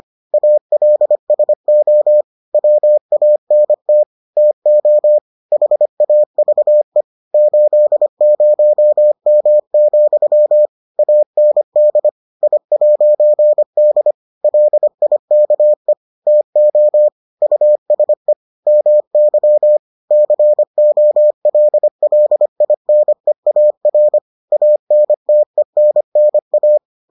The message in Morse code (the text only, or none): ALSO WANT TO HAVE 80M, AND I'D LIKE TO USE MY COLLINEAR ANTENNA